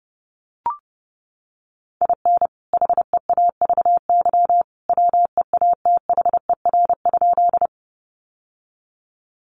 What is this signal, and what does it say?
Morse code, heard as a keyed tone, about 30 wpm, 700 Hz. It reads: IN HEAVY WEATHER?
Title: Sentences from Top 1100 Words 30wpm